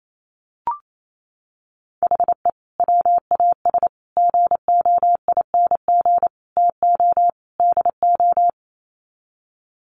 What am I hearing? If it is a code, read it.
Morse code, heard as a keyed tone, about 28 wpm, 700 Hz. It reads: HE WAS GOING TO DO
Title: Sets of 5 Words Encore2 28wpm